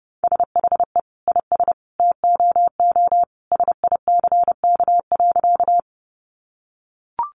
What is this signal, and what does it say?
Morse code, heard as a keyed tone, about 30 wpm, 700 Hz. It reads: SHE IS TOO SICK.
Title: Easy Phrases 30wpm